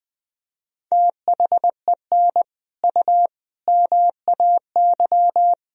Morse code, heard as a keyed tone, about 20 wpm, 700 Hz. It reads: THEN U MAY
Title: Sentences from Top 100 Words and Abbreviations 20wpm